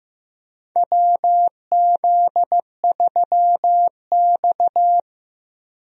Morse code, heard as a keyed tone, about 15 wpm, 700 Hz. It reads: WZ3X
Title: USA Call Signs - 4 Characters - Repeated 2x - 15wpm